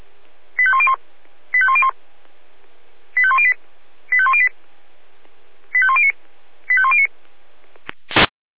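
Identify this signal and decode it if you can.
Five-tone selective call (ZVEI), heard as quick runs of five tones, tones of about 70 ms. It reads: 85101 85101 85108 85108 85109 85109
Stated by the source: Five-tone ZVEI selective call, received by SDR.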